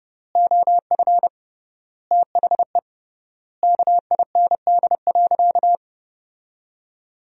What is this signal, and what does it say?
Morse code, heard as a keyed tone, about 32 wpm, 700 Hz. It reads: OF THE KIND.
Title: 3x Word-Spacing - Sentences from Top 500 Words - 30wpm